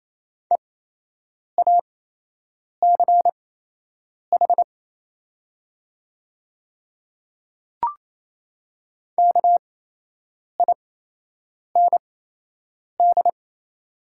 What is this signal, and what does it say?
Morse code, heard as a keyed tone, about 27 wpm, spaced as for 8 wpm, 700 Hz. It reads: EACH KIND
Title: 8x Character Spacing - Top 300 Words - 28wpm